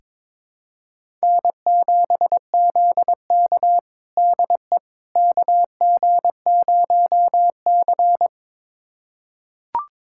Morse code, heard as a keyed tone, about 22 wpm, 700 Hz. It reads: N7ZK DE KG0C
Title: USA Callsign DE Callsign - 4 Characters - 22wpm